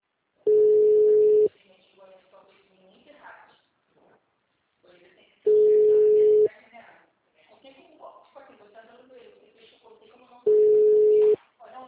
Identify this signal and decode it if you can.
Ringback tone (Europe, 425 Hz), with speech.